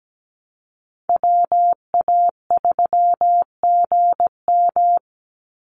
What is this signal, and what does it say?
Morse code, heard as a keyed tone, about 17 wpm, 700 Hz. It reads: WA3GM
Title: Call Signs Repeated 2x - 17wpm